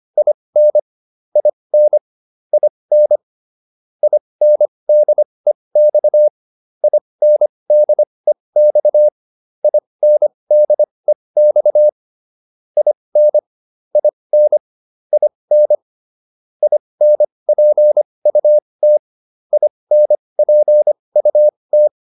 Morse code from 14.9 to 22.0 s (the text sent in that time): IN INPUT INPUT